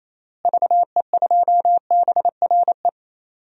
Morse code, heard as a keyed tone, about 28 wpm, 700 Hz. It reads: VE2BRE
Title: DXCC-100 Call Signs - 6 Characters - Repeated 3x - 28wpm